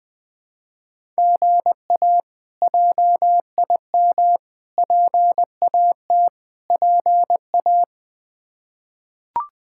Morse code, heard as a keyed tone, about 20 wpm, 700 Hz. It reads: GA JIM PAT PA
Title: K1USN SST QSO Practice - 20wpm